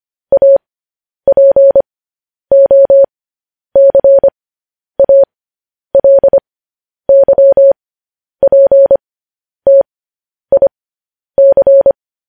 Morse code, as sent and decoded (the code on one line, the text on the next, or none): .- .--. --- -.-. .- .-.. -.-- .--. - .. -.-.
APOCALYPTIC